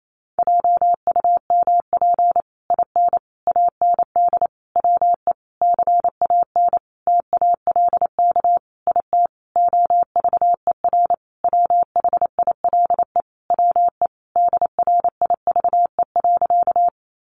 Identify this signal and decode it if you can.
Morse code, heard as a keyed tone, about 28 wpm, 700 Hz. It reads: JUMP IN AND WE CAN TALK IT OVER WHILE WE DRIVE.
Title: Sentences from Top 1300 Words 28wpm